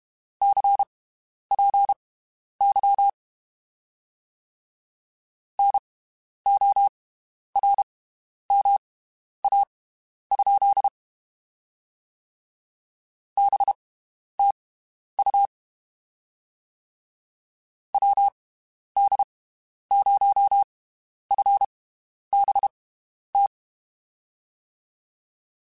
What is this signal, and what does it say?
Morse code, heard as a keyed tone, about 32 wpm, spaced as for 11 wpm, 800 Hz. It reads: CPY NORMA? BTU WD0FBT
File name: (spaced) / QSOs Spaced-mp3s / WD0FBT 1 @30 sp